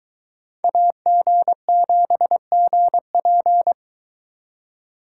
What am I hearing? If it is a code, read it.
Morse code, heard as a keyed tone, about 22 wpm, 700 Hz. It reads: AG7GP